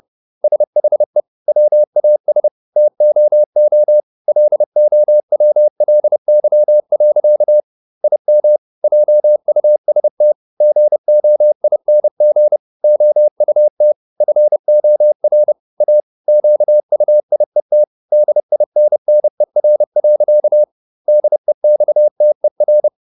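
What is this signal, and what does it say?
Morse code, heard as a keyed tone, about 30 wpm, 600 Hz. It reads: SHE WAS TOO LOWLY. IM JUST GOING OUT FOR A QUIET DINNER. DEXTER
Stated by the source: Morse code at 30 wpm, the novel Never by Ken Follett in English, from CW med Gnister.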